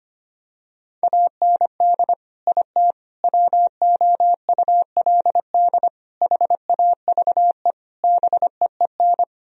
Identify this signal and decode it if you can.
Morse code, heard as a keyed tone, about 25 wpm, 700 Hz. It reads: AND IT WOULD HAVE BEEN
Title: Sentences from Top 1000 Words - Review Part 4 - 25wpm